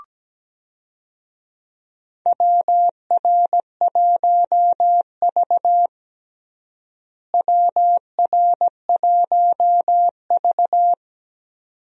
Morse code, heard as a keyed tone, about 17 wpm, 700 Hz. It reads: WR1V WR1V
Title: USA Call Signs - 4 Characters - Repeated 2x - 17wpm